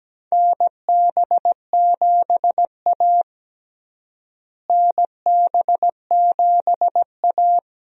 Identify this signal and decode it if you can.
Morse code, heard as a keyed tone, about 17 wpm, 700 Hz. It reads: NB7A NB7A
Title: USA Call Signs - 4 Characters - Repeated 2x - 17wpm